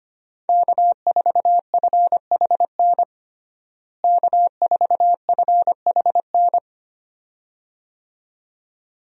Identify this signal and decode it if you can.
Morse code, heard as a keyed tone, about 25 wpm, 700 Hz. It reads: K4FHN K4FHN
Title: USA Call Signs - 5 Characters - Repeated 2x - 25wpm